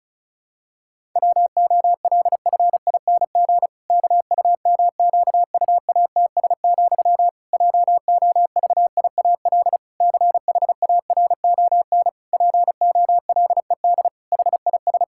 Morse code, heard as a keyed tone, about 36 wpm, 700 Hz. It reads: WOLFING KUMQUATS, JOVIAL CHARON POLED HIS
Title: Pangram Challenge v2 - 35wpm